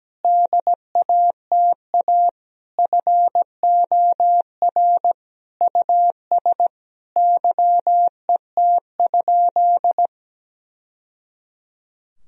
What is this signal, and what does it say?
Morse code, heard as a keyed tone, about 17 wpm, 700 Hz. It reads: DATA FOR US YET?
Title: Sentences from Top 1000 Words - Review Part 3 - 17wpm